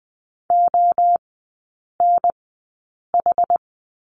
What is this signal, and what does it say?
Morse code, heard as a keyed tone, about 20 wpm, spaced as for 8 wpm, 700 Hz. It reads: ONH
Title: 067 Lesson 12e - Letter d - Three Characters 20wpm